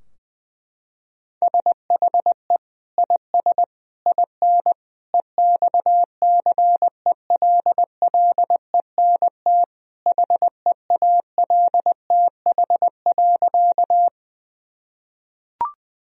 Morse code, heard as a keyed tone, about 20 wpm, 700 Hz. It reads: SHE IS IN EXCELLENT HEALTH.
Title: Sentences from Top 1000 Words - Review Part 3 - 20wpm